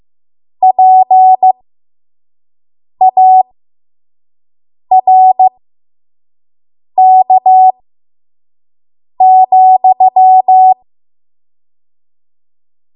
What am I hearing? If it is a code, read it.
Morse code, heard as a keyed tone, about 15 wpm, spaced as for 5 wpm, 750 Hz. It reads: PARK,
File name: Archive / 5 WPM / 05WPM